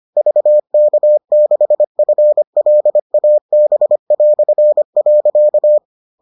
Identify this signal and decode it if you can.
Morse code, heard as a keyed tone, about 25 wpm, 600 Hz. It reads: VK6FLAB".